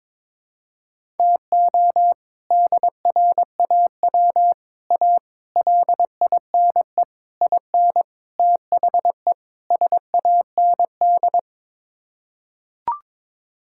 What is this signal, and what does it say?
Morse code, heard as a keyed tone, about 22 wpm, 700 Hz. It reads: TO DRAW A LINE IN THE SAND
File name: practice / 22wpm / Sets of 7 Words - 22wpm